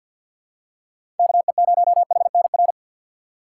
Morse code, heard as a keyed tone, about 52 wpm, 700 Hz. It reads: KE0HNR
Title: USA Call Signs - 6 Characters - 4x Character Spacing - 50wpm